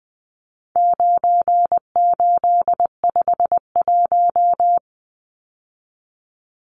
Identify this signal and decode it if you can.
Morse code, heard as a keyed tone, about 20 wpm, 700 Hz. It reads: 9851
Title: Numbers - 4-Digit 20wpm